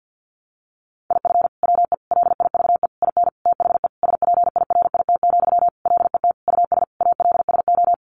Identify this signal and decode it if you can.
Morse code, heard as a keyed tone, about 50 wpm, 700 Hz. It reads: IF WE LIVE IN THE SPIRIT, LET US ALSO
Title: Sentences from Top 1400 Words 50wpm